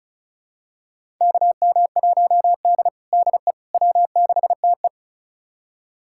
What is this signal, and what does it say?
Morse code, heard as a keyed tone, about 36 wpm, 700 Hz. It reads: KM1D DE W6TE
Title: USA Callsign DE Callsign - 4 Characters - 35wpm